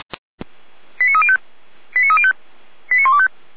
Five-tone selective call (ZVEI), heard as quick runs of five tones, tones of about 70 ms. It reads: 89305 89305 89125